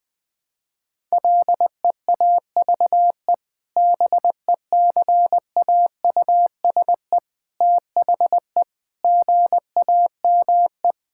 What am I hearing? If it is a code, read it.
Morse code, heard as a keyed tone, about 19 wpm, 700 Hz. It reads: LEAVE BECAUSE THE GAME
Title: Sentences with 2-Word Phrases - Spoken Beginning - 20wpm